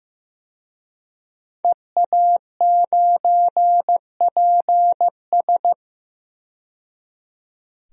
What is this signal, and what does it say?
Morse code, heard as a keyed tone, about 15 wpm, 700 Hz. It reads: EA9PS
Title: DXCC-100 Call Signs - 5 Characters - 15wpm